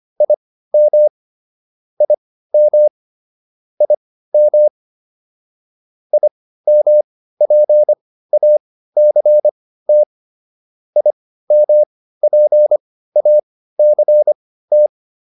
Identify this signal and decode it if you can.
Morse code, heard as a keyed tone, about 25 wpm, spaced as for 15 wpm, 600 Hz. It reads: IM IM IM IMPACT IMPACT